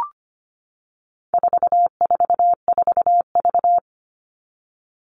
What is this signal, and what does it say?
Morse code, heard as a keyed tone, about 25 wpm, 700 Hz. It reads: 444V
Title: All Cut Numbers - 4-Digit 25wpm